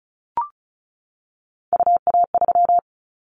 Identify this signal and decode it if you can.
Morse code, heard as a keyed tone, about 34 wpm, 700 Hz. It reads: UA3
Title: Callsign Prefixes for Top 100 DXCC - 35wpm